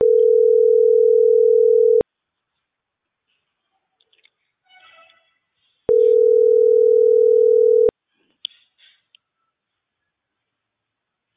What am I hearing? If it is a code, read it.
Ringback tone (North America), 440 + 480 Hz.